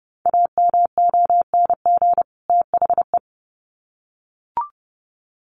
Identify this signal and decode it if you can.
Morse code, heard as a keyed tone, about 30 wpm, 700 Hz. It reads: AMONG THE
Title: Sets of 2 Words 30wpm